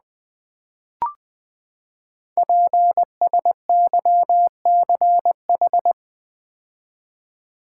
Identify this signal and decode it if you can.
Morse code, heard as a keyed tone, about 20 wpm, 700 Hz. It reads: PSYCH
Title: Build A Word - 5 Letters At A Time - 12-Letter Words - 20wpm